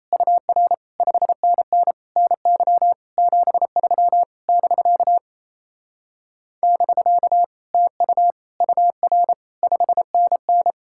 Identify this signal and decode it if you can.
Morse code, heard as a keyed tone, about 38 wpm, 700 Hz. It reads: UR 5NN NY 73 <BK> <BK> TU UR 5NN